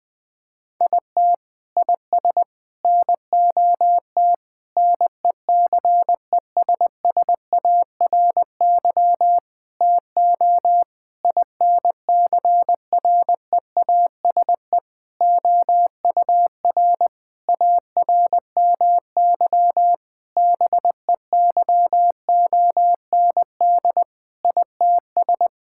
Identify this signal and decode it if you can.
Morse code, heard as a keyed tone, about 20 wpm, 700 Hz. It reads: IT IS NOT NECESSARY TO INCREASE OUR ARMY BEYOND ITS
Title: Sentences from Top 1000 Words - Review Part 6 - 20wpm